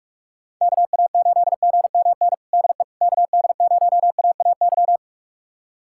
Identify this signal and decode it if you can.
Morse code, heard as a keyed tone, about 46 wpm, 700 Hz. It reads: KA8GMN DE KD0AAY
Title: USA Callsign DE Callsign 45wpm